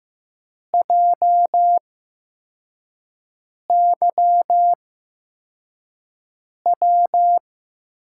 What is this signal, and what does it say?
Morse code, heard as a keyed tone, about 15 wpm, spaced as for 4 wpm, 700 Hz. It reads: JYW
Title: USA Call Signs - 6 Characters - 8x Character Spacing - 15wpm